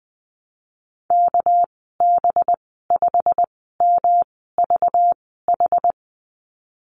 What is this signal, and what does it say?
Morse code, heard as a keyed tone, about 19 wpm, spaced as for 14 wpm, 700 Hz. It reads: KB5MVH
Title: 200 Lesson 32g - Letter k - Call Signs 20wpm